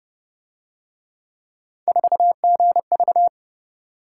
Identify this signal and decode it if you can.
Morse code, heard as a keyed tone, about 28 wpm, 700 Hz. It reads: 4GV